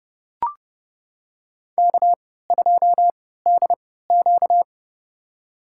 Morse code, heard as a keyed tone, about 30 wpm, spaced as for 17 wpm, 700 Hz. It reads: K2DQ
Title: USA Call Signs - 4 Characters - 3x Character Spacing - 30wpm